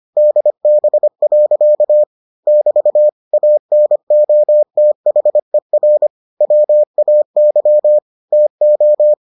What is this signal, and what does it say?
Morse code, heard as a keyed tone, about 25 wpm, 600 Hz. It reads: DB. <BT> ANOTHER WAY TO